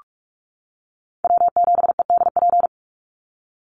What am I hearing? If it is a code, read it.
Morse code, heard as a keyed tone, about 44 wpm, 700 Hz. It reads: W7EDP
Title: USA Call Signs 45wpm